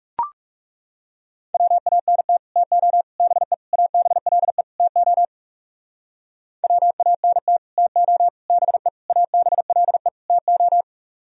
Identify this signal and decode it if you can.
Morse code, heard as a keyed tone, about 36 wpm, 700 Hz. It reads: WANT TO BE ABLE TO WANT TO BE ABLE TO
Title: Speed-Racing - Sets of 5 Words 30wpm